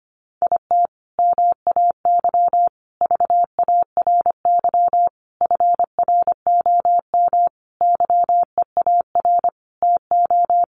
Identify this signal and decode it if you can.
Morse code, heard as a keyed tone, about 25 wpm, 700 Hz. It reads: IT MAY VARY FROM YEAR TO
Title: Sentences from Top 1200 Words 25wpm